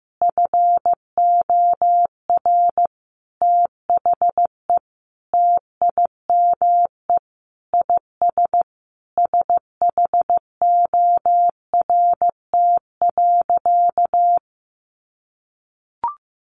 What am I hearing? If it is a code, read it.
Morse code, heard as a keyed tone, about 15 wpm, 700 Hz. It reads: FOR THE TIME IS SHORT.
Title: Sentences from Top 500 Words 15wpm